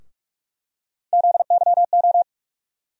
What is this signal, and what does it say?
Morse code, heard as a keyed tone, about 43 wpm, 700 Hz. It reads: ZYO
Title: USA Call Sign Postfixes - 3 Letters 45wpm